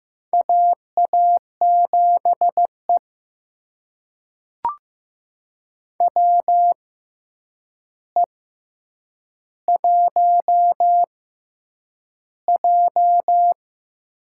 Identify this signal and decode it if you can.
Morse code, heard as a keyed tone, about 15 wpm, spaced as for 8 wpm, 700 Hz. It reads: AA7E WE1J
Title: USA Call Signs - 4 Characters - 6x Character Spacing - 15wpm